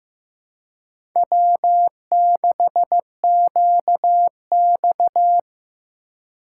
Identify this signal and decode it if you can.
Morse code, heard as a keyed tone, about 15 wpm, 700 Hz. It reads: W6QX
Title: USA Call Signs - 4 Characters - 4x Character Spacing - 15wpm